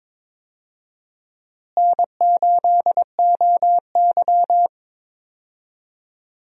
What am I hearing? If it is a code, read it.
Morse code, heard as a keyed tone, about 22 wpm, 700 Hz. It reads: N8OY
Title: USA Call Signs - 4 Characters - Repeated 2x - 22wpm